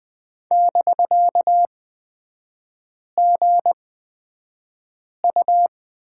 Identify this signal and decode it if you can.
Morse code, heard as a keyed tone, about 20 wpm, spaced as for 5 wpm, 700 Hz. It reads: <BK>GU
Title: 250 Lesson 40e - BK - Three Characters 20wpm